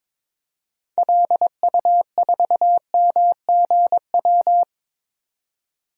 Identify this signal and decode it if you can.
Morse code, heard as a keyed tone, about 22 wpm, 700 Hz. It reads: LU4MGW
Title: DXCC-100 Call Signs - 6 Characters - 22wpm